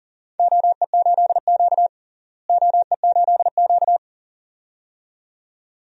Morse code, heard as a keyed tone, about 37 wpm, 700 Hz. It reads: OE8Q OE8Q
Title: DXCC-100 Call Signs - 4 Characters - Repeated 2x - 40wpm